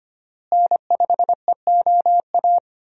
Morse code, heard as a keyed tone, about 25 wpm, 700 Hz. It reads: N5EOA
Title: USA Call Signs - 5 Characters - Repeated 2x - 25wpm